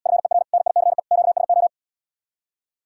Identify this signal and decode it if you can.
Morse code, heard as a keyed tone, about 79 wpm, 700 Hz. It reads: FER NICE QSO